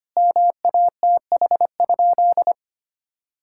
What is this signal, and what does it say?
Morse code, heard as a keyed tone, about 25 wpm, 700 Hz. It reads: MATH?
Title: Sentences from Top 1000 Words - Review Part 6 - 25wpm